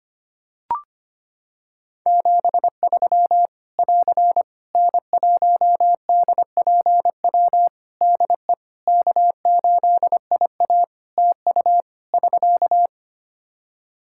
Morse code, heard as a keyed tone, about 25 wpm, 700 Hz. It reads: 73 <AR> N1DPW DE K8IA TU <SK>
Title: K1USN SST QSO Practice 25wpm